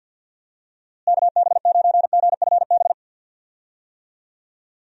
Morse code, heard as a keyed tone, about 52 wpm, 700 Hz. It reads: KB9GFB
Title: USA Call Signs - 6 Characters - Repeated 2x - 50wpm